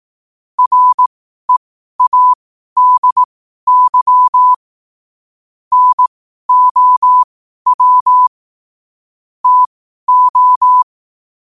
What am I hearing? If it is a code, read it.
Morse code, heard as a keyed tone, about 18 wpm, spaced as for 12 wpm, 1000 Hz. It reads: READY NOW TO
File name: qotdfiles / 12wpm